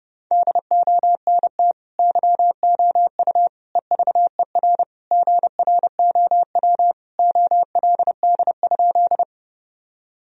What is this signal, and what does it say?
Morse code, heard as a keyed tone, about 30 wpm, 700 Hz. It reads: DONT YOU EVER GROW OLD?
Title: Sentences from Top 1000 Words - Review Part 1 - 30wpm